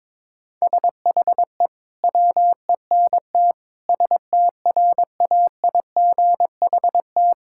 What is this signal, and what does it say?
Morse code, heard as a keyed tone, about 22 wpm, 700 Hz. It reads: SHE WENT STRAIGHT
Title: Sentences from Top 1000 Words - Review Part 6 - 22wpm